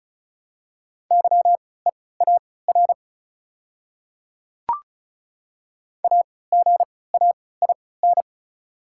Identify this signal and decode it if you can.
Morse code, heard as a keyed tone, about 36 wpm, spaced as for 20 wpm, 700 Hz. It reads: YEAR AGAIN
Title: 3x Character Spacing - Top 200 Words - 35wpm